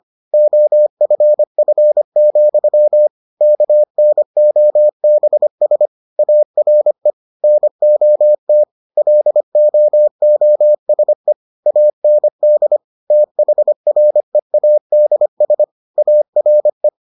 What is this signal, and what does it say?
Morse code, heard as a keyed tone, about 25 wpm, 600 Hz. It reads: OFF, KNOBS ARE NOT LOOSE AND THREADS ARE